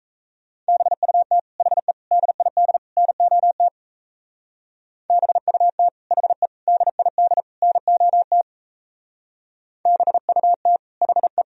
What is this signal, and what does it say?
Morse code, heard as a keyed tone, about 33 wpm, 700 Hz. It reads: BUT HE DID NOT BUT HE DID NOT BUT HE
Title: Speed-Racing - Sets of 4 Words 28wpm